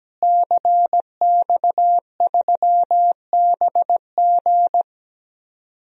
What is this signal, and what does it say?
Morse code, heard as a keyed tone, about 17 wpm, 700 Hz. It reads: CX3BG
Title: DXCC-100 Call Signs - 5 Characters - Repeated 2x - 17wpm